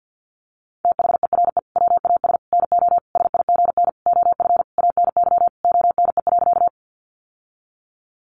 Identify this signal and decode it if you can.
Morse code, heard as a keyed tone, about 50 wpm, 700 Hz. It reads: THERE WAS NO SIGN OF ANY ONE.
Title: Sentences from Top 1000 Words 50wpm